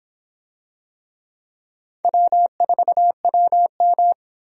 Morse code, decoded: W4WM

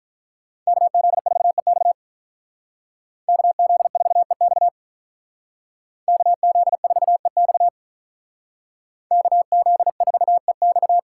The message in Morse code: KZ4EX KZ4EX KZ4EX KZ4EX

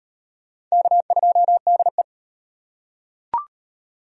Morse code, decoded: K2DE

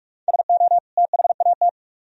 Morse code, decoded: SO THAT

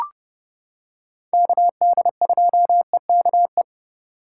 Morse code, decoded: KD2EKE

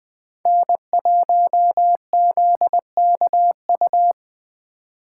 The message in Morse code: N1ZKU

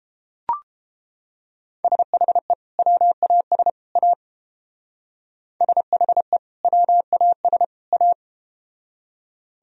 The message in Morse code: SHE WAS A SHE WAS A

